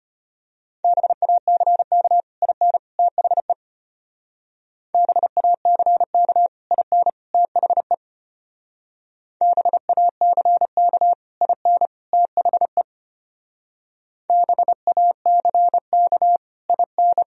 Morse code, heard as a keyed tone, about 29 wpm, 700 Hz. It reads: BACK IN THE BACK IN THE BACK IN THE BACK IN
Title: Speed-Racing - Sets of 3 Words 25wpm